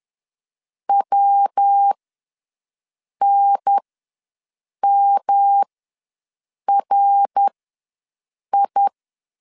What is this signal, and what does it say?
Morse code, heard as a keyed tone, about 10 wpm, spaced as for 6 wpm, 800 Hz. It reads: WNMRI
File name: LEZIONI / 5 LBVC / 5 L ws 10wpm 3636